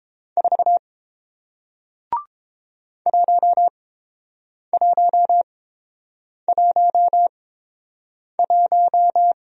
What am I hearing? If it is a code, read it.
Morse code, heard as a keyed tone, about 30 wpm, spaced as for 7 wpm, 700 Hz. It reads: 4 1111